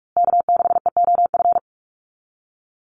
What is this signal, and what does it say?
Morse code, heard as a keyed tone, about 44 wpm, 700 Hz. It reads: K6EOF
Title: USA Call Signs Repeated 3x 45wpm